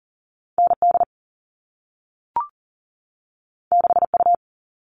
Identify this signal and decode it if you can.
Morse code, heard as a keyed tone, about 40 wpm, 700 Hz. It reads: ND 6U
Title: All Cut Numbers - 2-Digit 40wpm